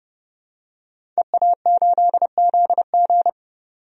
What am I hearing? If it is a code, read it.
Morse code, heard as a keyed tone, about 30 wpm, 700 Hz. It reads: EA8ZG
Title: DXCC-100 Call Signs - 5 Characters - 30wpm